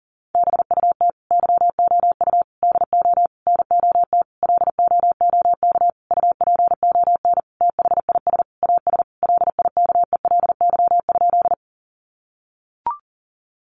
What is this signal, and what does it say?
Morse code, heard as a keyed tone, about 40 wpm, 700 Hz. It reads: BUT YOU DO NOT LOOK UPON THIS AS LIKELY?
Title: Sentences from Top 1100 Words 40wpm